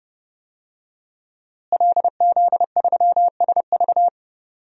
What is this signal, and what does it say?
Morse code, heard as a keyed tone, about 30 wpm, 700 Hz. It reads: LZ3SV
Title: DXCC-100 Call Signs - 5 Characters - Repeated 2x - 30wpm